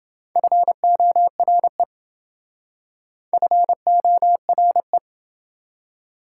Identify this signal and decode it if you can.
Morse code, heard as a keyed tone, about 29 wpm, 700 Hz. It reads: FORE FORE